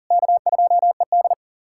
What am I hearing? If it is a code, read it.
Morse code, heard as a keyed tone, about 40 wpm, 690 Hz. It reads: K2ED
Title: USA Callsign DE Callsign - 4 Characters - 40wpm